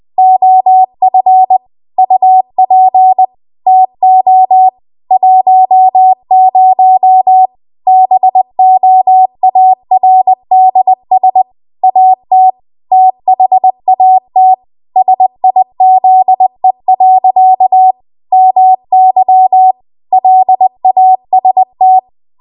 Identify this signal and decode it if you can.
Morse code, heard as a keyed tone, about 20 wpm, 750 Hz. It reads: OF UP TO 10 BOARDS AT THAT SIZE. MY LAST
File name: Archive / 20 WPM / 20WPM